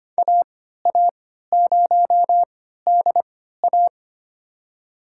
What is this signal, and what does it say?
Morse code, heard as a keyed tone, about 25 wpm, spaced as for 14 wpm, 700 Hz. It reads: AA0DA
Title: USA Call Signs - 5 Characters - 3x Character Spacing - 25wpm